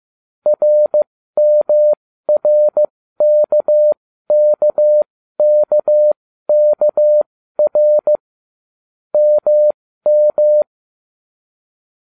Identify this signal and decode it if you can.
Morse code, heard as a keyed tone, about 15 wpm, spaced as for 13 wpm, 600 Hz. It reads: RMRKKKKR MM